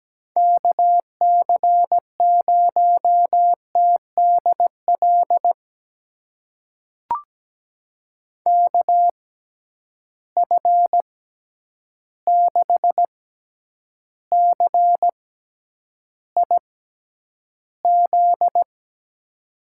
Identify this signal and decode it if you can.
Morse code, heard as a keyed tone, about 17 wpm, spaced as for 9 wpm, 700 Hz. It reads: KC0TDL KF6CIZ
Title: USA Call Signs - 6 Characters - 6x Character Spacing - 17wpm